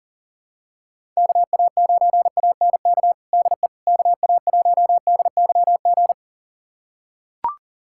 Morse code, heard as a keyed tone, about 40 wpm, 700 Hz. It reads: KA9ANK DE KA1DYG
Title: USA Callsign DE Callsign - 6 Characters - 40wpm